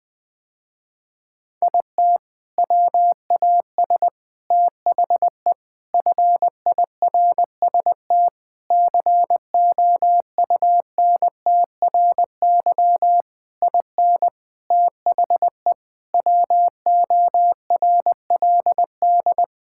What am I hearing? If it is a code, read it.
Morse code, heard as a keyed tone, about 20 wpm, 700 Hz. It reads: IT WAS THE FIRST COUNTRY IN THE WORLD
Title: Sentences with 7-Word Phrases - Spoken Beginning - 20wpm